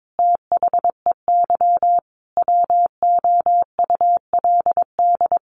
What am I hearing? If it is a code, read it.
Morse code, heard as a keyed tone, about 22 wpm, 700 Hz. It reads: THEY WOULD
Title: Sentences from Top 1500 Words 22wpm